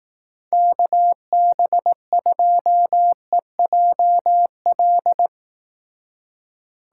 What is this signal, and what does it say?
Morse code, heard as a keyed tone, about 18 wpm, 700 Hz. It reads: KB2EJL
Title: Speed Racing - USA 2x3 Call Signs 15wpm